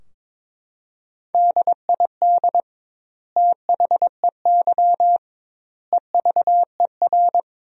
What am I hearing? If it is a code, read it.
Morse code, heard as a keyed tone, about 22 wpm, 700 Hz. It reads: DID THEY EVER
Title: 2x Word-Spacing - Sentences from Top 400 Words - 22wpm